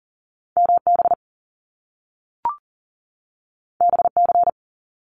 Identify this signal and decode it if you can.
Morse code, heard as a keyed tone, about 40 wpm, 700 Hz. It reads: MB BC
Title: Canadian Provinces and Territories 40wpm